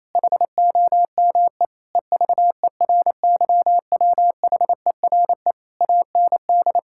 Morse code, heard as a keyed tone, about 28 wpm, 700 Hz. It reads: HOME EVERYWHERE AND
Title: Sentences from Top 1600 Words 28wpm